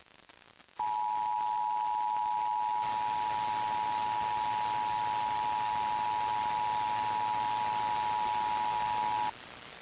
Emergency Alert System (SAME) tones (the 853 and 960 Hz attention tone), with noise. It is heard from 0.7 to 9.3 s.
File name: eas / KKPRKXPN / KKPRKXPN